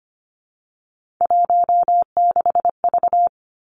Morse code, heard as a keyed tone, about 25 wpm, 700 Hz. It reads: 16V
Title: Three Challenging Letter-Number Combinations 25wpm